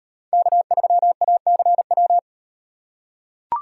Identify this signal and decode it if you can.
Morse code, heard as a keyed tone, about 36 wpm, 700 Hz. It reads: K3ACW